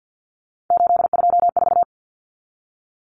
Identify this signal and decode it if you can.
Morse code, heard as a keyed tone, about 49 wpm, 700 Hz. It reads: 824
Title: Numbers - 3-Digit 50wpm